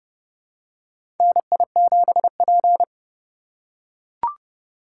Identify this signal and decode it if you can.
Morse code, heard as a keyed tone, about 32 wpm, 700 Hz. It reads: NI7P
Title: USA Call Signs - 2x Character Spacing - 30wpm